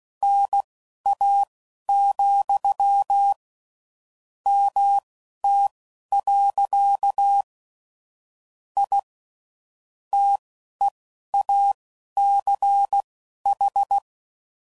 Morse code, heard as a keyed tone, about 16 wpm, spaced as for 11 wpm, 800 Hz. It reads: NA, MT. I TEACH
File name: CW Code Groups @15WPM / QSOs / WD0BGP Spaced